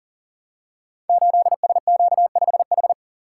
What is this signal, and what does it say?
Morse code, heard as a keyed tone, about 43 wpm, 700 Hz. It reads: 8SQ5H
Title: Code Groups - 5 Letter:Numbers - 40wpm